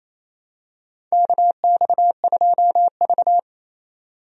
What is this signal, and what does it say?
Morse code, heard as a keyed tone, about 28 wpm, 700 Hz. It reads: KX2V
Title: USA Call Signs - 4 Characters - 28wpm